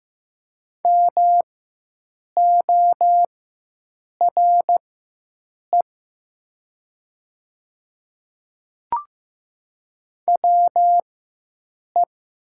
Morse code, heard as a keyed tone, about 15 wpm, spaced as for 7 wpm, 700 Hz. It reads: MORE WE
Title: 4x Character Spacing - Top 300 Words - 15wpm